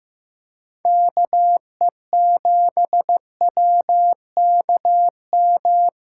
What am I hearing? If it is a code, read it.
Morse code, heard as a keyed tone, about 16 wpm, 700 Hz. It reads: KE7WKM
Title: USA Callsign DE Callsign - 6 Characters - 15wpm